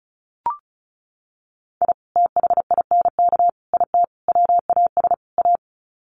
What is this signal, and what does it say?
Morse code, heard as a keyed tone, about 34 wpm, 700 Hz. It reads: I THINK IT WAS A
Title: Sets of 5 Words 35wpm